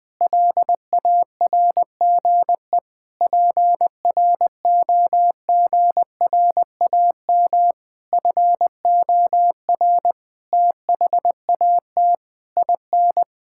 Morse code, heard as a keyed tone, about 20 wpm, 700 Hz. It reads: LARGE PROGRAM FOR THAT IN
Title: Sentences from Top 1000 Words - Review Part 5 - 20wpm